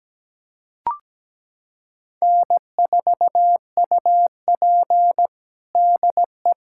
Morse code, heard as a keyed tone, about 17 wpm, 700 Hz. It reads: N4UP DE
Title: USA Callsign DE Callsign - 4 Characters - 17wpm